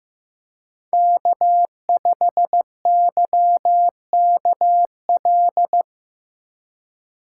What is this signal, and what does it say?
Morse code, heard as a keyed tone, about 16 wpm, 700 Hz. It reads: K5YKL
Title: USA Call Signs - 5 Characters - 2x Character Spacing - 15wpm